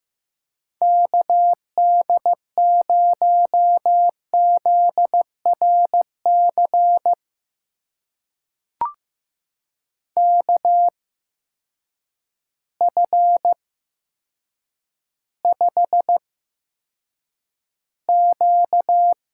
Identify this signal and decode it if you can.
Morse code, heard as a keyed tone, about 15 wpm, spaced as for 8 wpm, 700 Hz. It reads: KD0ZRC KF5Q
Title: USA Call Signs - 6 Characters - 8x Character Spacing - 15wpm